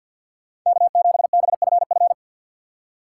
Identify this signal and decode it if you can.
Morse code, heard as a keyed tone, about 48 wpm, 700 Hz. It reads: K7BFF
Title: USA Call Signs - 4x Character Spacing - 50wpm